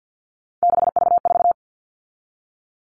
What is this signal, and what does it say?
Morse code, heard as a keyed tone, about 50 wpm, 700 Hz. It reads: 6V4
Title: All Cut Numbers - 3-Digit 50wpm